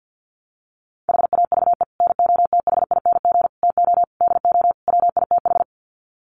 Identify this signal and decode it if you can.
Morse code, heard as a keyed tone, about 50 wpm, 700 Hz. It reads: HAVE NOTHING TO DO WITH
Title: Sets of 5 Words 50wpm